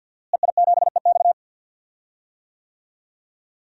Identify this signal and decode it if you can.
Morse code, heard as a keyed tone, about 46 wpm, 700 Hz. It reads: EI7EX
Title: DXCC-100 Call Signs - 5 Characters - Repeated 3x - 50wpm